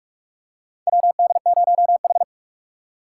Morse code, heard as a keyed tone, about 46 wpm, 700 Hz. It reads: WD0H
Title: USA Call Signs - 4 Characters - 2x Character Spacing - 45wpm